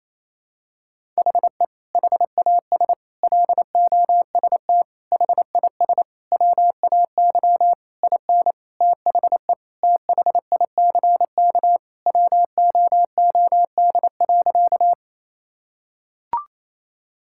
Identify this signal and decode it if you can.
Morse code, heard as a keyed tone, about 28 wpm, 700 Hz. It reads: HE HAS LOST HIS WAY IN THE THICK WOOD.
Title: Sentences from Top 1600 Words 28wpm